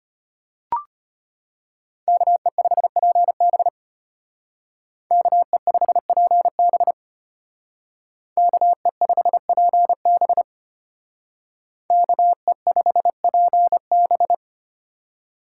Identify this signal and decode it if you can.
Morse code, heard as a keyed tone, about 31 wpm, 700 Hz. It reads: KE5PB KE5PB KE5PB KE5PB